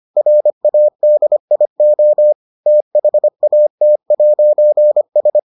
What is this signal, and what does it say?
Morse code, heard as a keyed tone, about 25 wpm, 600 Hz. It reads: RADIO THAT'S